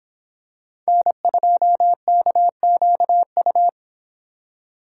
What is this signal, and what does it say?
Morse code, heard as a keyed tone, about 26 wpm, 700 Hz. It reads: N2KQU